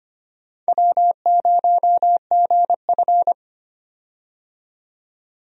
Morse code, heard as a keyed tone, about 25 wpm, 700 Hz. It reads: W0GF
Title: USA Call Signs - 4 Characters - 25wpm